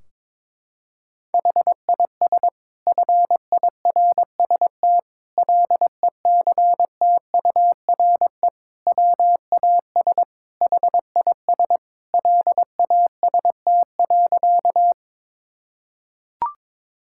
Morse code, heard as a keyed tone, about 22 wpm, 700 Hz. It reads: HIS FIRST LECTURE WAS HIS LAST.
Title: Sentences from Top 1600 Words 22wpm